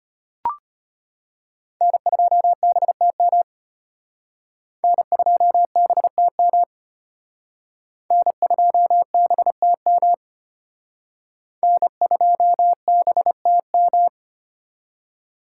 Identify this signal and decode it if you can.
Morse code, heard as a keyed tone, about 33 wpm, 700 Hz. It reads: N2BTM N2BTM N2BTM N2BTM